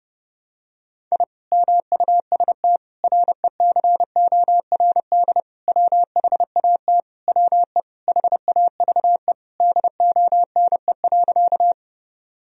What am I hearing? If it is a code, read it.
Morse code, heard as a keyed tone, about 30 wpm, 700 Hz. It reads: I MUST RECORD WHAT WE HAVE DONE.
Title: Sentences from Top 1000 Words - Review Part 3 - 30wpm